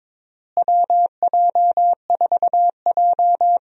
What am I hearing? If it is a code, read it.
Morse code, heard as a keyed tone, about 22 wpm, 700 Hz. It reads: WJ4J
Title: USA Call Signs - 4 Characters - Repeated 3x - 22wpm